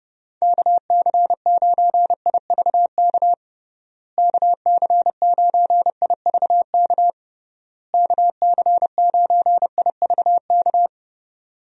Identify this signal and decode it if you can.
Morse code, heard as a keyed tone, about 32 wpm, 700 Hz. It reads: KC9IVK KC9IVK KC9IVK
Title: USA Call Signs - 6 Characters - Repeated 3x - 30wpm